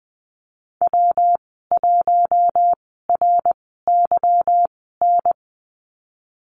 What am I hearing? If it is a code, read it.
Morse code, heard as a keyed tone, about 20 wpm, spaced as for 15 wpm, 700 Hz. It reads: W1RYN
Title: 147 Lesson 24g - Letter y - Call Signs 20wpm